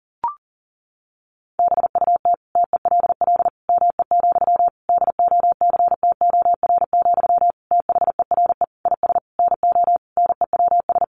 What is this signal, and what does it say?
Morse code, heard as a keyed tone, about 40 wpm, 700 Hz. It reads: BUT TELL ME, DOCTOR, THERE IS NO NEWS
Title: Sentences from Top 1500 Words 40wpm